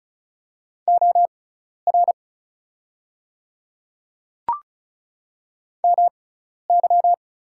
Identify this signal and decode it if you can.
Morse code, heard as a keyed tone, about 36 wpm, spaced as for 12 wpm, 700 Hz. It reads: OR MY
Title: 6x Character Spacing - Top 500 Words - 35wpm